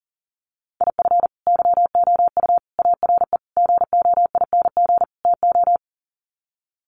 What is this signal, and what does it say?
Morse code, heard as a keyed tone, about 40 wpm, 700 Hz. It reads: IF YOU ARE GOING TO
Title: Sets of 5 Words Encore2 40wpm